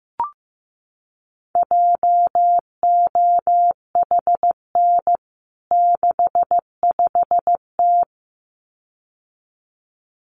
Morse code, heard as a keyed tone, about 15 wpm, 700 Hz. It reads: JOHN 65T
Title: CWT Exchange 15wpm